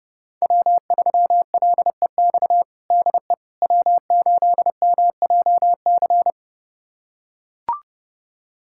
Morse code, heard as a keyed tone, about 30 wpm, 700 Hz. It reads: W3LEX DE W8MJC
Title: USA Callsign DE Callsign - 5 Characters - 30wpm